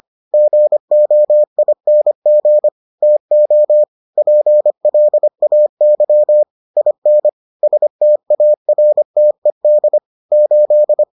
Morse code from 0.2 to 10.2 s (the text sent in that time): GOING TO PLAY IN STARTED